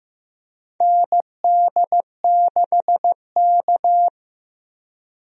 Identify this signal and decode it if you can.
Morse code, heard as a keyed tone, about 15 wpm, 700 Hz. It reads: ND6K
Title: USA Call Signs - 4 Characters - Repeated 3x - 15wpm